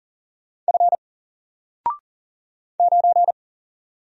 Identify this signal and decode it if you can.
Morse code, heard as a keyed tone, about 40 wpm, 700 Hz. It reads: F 9
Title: Single Character (Letter:Number:Punctuation:Prosign) 40wpm